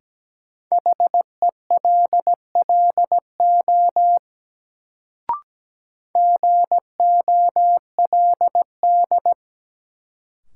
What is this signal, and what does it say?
Morse code, heard as a keyed tone, about 17 wpm, 700 Hz. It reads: HELLO GOLD
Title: Sentences from Top 1100 Words - New Word Drill Alt - 17wpm